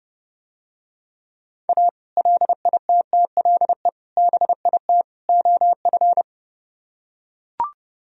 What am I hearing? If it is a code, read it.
Morse code, heard as a keyed tone, about 32 wpm, 700 Hz. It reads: A LITTLE BIT OF